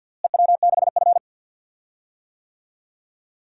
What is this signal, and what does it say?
Morse code, heard as a keyed tone, about 51 wpm, 700 Hz. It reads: EW6F
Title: DXCC-100 Call Signs - 4 Characters - Repeated 3x - 50wpm